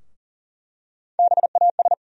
Morse code, heard as a keyed tone, about 40 wpm, 700 Hz. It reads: BAS
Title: USA Call Sign Postfixes - 3 Letters 40wpm